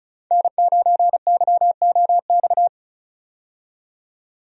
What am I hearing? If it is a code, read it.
Morse code, heard as a keyed tone, about 36 wpm, 700 Hz. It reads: N9YOX
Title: USA Call Signs - 5 Characters - 35wpm